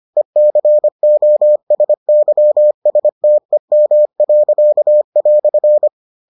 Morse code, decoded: ECOSYSTEM."

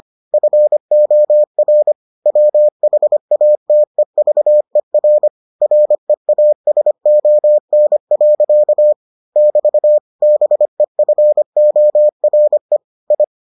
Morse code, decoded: FOR WHATEVER REASON. <BT> BEFORE I

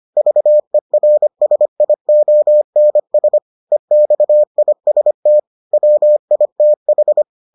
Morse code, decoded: VERSIONS EXIST WITH

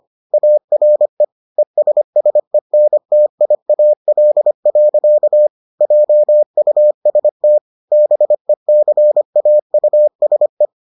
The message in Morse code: ARE ESSENTIAL. JUST BECAUSE

